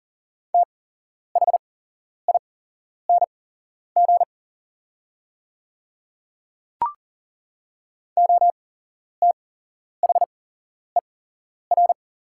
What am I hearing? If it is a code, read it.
Morse code, heard as a keyed tone, about 40 wpm, spaced as for 11 wpm, 700 Hz. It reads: THING OTHER
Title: 8x Character Spacing - Top 100 Words - 40wpm